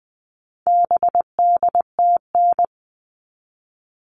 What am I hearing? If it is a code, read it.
Morse code, heard as a keyed tone, about 20 wpm, 700 Hz. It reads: BDTN
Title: All Cut Numbers - 4-Digit 20wpm